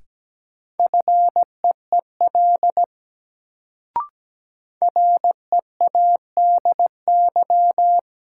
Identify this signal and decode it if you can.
Morse code, heard as a keyed tone, about 17 wpm, 700 Hz. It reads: FEEL READY
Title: Sentences from Top 400 Words - New Word Drill - 17wpm